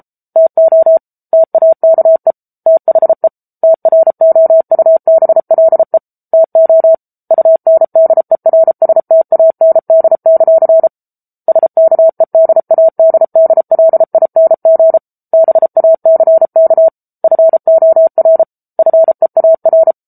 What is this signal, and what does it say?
Morse code, heard as a keyed tone, about 34 wpm, 650 Hz. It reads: TO TAKE THE TROUBLE TO UNDERSTAND; SKEDADDLING BACK FOR FEAR
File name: Book Two / 07-Chapter 7 - 33 WPM